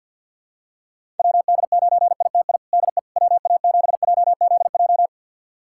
Morse code, heard as a keyed tone, about 50 wpm, 700 Hz. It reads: WD9ITI DE WA7JZJ
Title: USA Callsign DE Callsign - 6 Characters - 50wpm